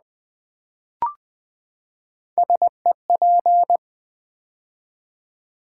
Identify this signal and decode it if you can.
Morse code, heard as a keyed tone, about 20 wpm, 700 Hz. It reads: SEP
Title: Build A Word - 3 Letters At A Time - 9-Letter Words - 20wpm